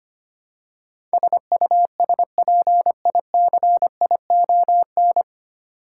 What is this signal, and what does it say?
Morse code, heard as a keyed tone, about 25 wpm, 700 Hz. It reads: SUSPICION
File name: practice / 25wpm / 9 Letter Words - First 3 Letters Given - 25wpm